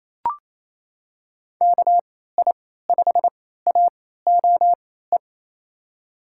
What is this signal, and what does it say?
Morse code, heard as a keyed tone, about 28 wpm, spaced as for 16 wpm, 700 Hz. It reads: KI5AOE
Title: USA Call Signs - 6 Characters - 3x Character Spacing - 28wpm